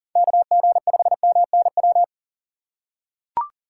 Morse code, heard as a keyed tone, about 40 wpm, 700 Hz. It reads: KG5MNW
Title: USA Call Signs - 6 Characters - 3x Character Spacing - 40wpm